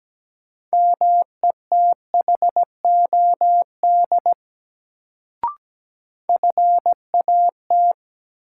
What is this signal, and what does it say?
Morse code, heard as a keyed tone, about 17 wpm, 700 Hz. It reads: METHOD FAT
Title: Sentences from Top 600 Words - New Word Drill - 17wpm